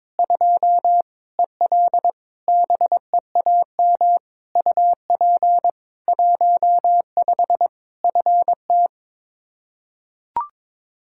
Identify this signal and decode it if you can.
Morse code, heard as a keyed tone, about 21 wpm, 700 Hz. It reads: 2 EL BEAM UP 15 FT
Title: Practice QSO - Exchange 2 - Info - 22wpm